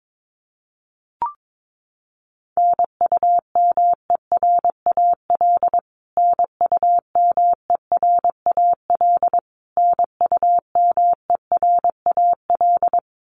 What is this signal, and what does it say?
Morse code, heard as a keyed tone, about 22 wpm, 700 Hz. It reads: NUMERAL NUMERAL NUMERAL
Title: Top 500 Words Repeated 22wpm